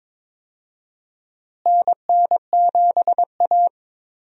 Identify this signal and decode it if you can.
Morse code, heard as a keyed tone, about 22 wpm, 700 Hz. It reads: NN7A
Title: USA Call Signs - 4 Characters - Repeated 2x - 22wpm